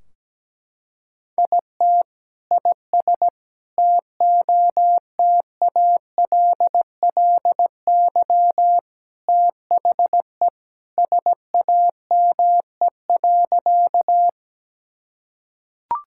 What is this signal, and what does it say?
Morse code, heard as a keyed tone, about 17 wpm, 700 Hz. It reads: IT IS TOTALLY THE SAME.
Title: Sentences from Top 1600 Words 17wpm